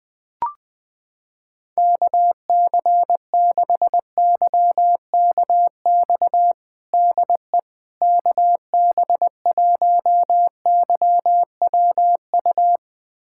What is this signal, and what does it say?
Morse code, heard as a keyed tone, about 20 wpm, 700 Hz. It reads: KC6YKX DE KB1YWU
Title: USA Callsign DE Callsign - 6 Characters - 20wpm